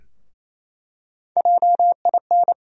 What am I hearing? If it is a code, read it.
Morse code, heard as a keyed tone, about 28 wpm, 700 Hz. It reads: JIN
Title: USA Call Sign Postfixes - 3 Letters 28wpm